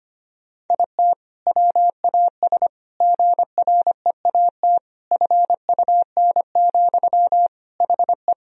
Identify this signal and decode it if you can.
Morse code, heard as a keyed tone, about 25 wpm, 700 Hz. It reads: IT WAS GREAT FUN, HE
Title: Sentences from Top 1000 Words - Review Part 4 - 25wpm